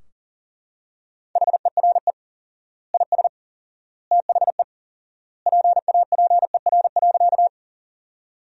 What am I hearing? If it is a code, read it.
Morse code, heard as a keyed tone, about 40 wpm, 700 Hz. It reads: HERE IS THE PAPER.
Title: 4x Word-Spacing - Sentences from Top 300 Words - 40wpm